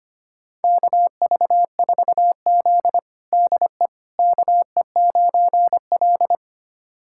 Morse code, heard as a keyed tone, about 25 wpm, 700 Hz. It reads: KV4Z DE KE9L
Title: USA Callsign DE Callsign - 4 Characters - 25wpm